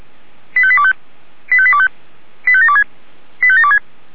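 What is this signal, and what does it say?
Five-tone selective call (ZVEI), heard as quick runs of five tones, tones of about 70 ms. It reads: 85825 85825 85726 85726
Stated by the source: SDR-received ZVEI five-tone selective call, code 85827.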